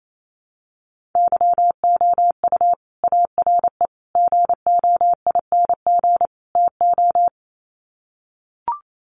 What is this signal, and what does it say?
Morse code, heard as a keyed tone, about 28 wpm, 700 Hz. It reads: YOU ARE GOING TO
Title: Sets of 4 Words 28wpm